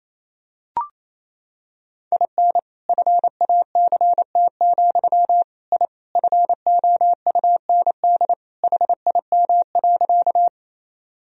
Morse code, heard as a keyed tone, about 28 wpm, 700 Hz. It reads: IN FACT, I FOUND HIM.
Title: Sentences from Top 1000 Words - Review Part 3 - 28wpm